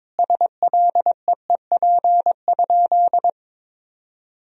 Morse code, heard as a keyed tone, about 21 wpm, 700 Hz. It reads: SLEEP?
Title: Sentences from Top 500 Words and Abbreviations 22wpm